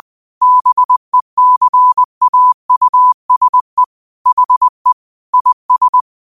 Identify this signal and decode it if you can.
Morse code, heard as a keyed tone, about 20 wpm, 1000 Hz. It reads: BECAUSE HE IS
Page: New quotes every day in morse code at 20 Words per minute.